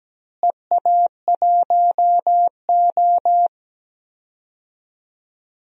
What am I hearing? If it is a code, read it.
Morse code, heard as a keyed tone, about 17 wpm, 700 Hz. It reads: EA1O
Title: DXCC-100 Call Signs - 4 Characters - Repeated 2x - 17wpm